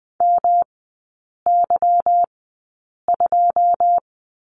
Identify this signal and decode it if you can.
Morse code, heard as a keyed tone, about 20 wpm, spaced as for 8 wpm, 700 Hz. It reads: MY2
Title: 152 Lesson 25e - Letter p - Three Characters 20wpm